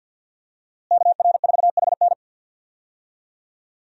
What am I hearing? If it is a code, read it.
Morse code, heard as a keyed tone, about 52 wpm, 700 Hz. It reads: KR4HN
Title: USA Call Signs - 5 Characters - Repeated 3x - 50wpm